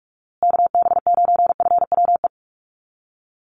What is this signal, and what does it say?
Morse code, heard as a keyed tone, about 44 wpm, 700 Hz. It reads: KB9FWE
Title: USA Call Signs Repeated 2x 45wpm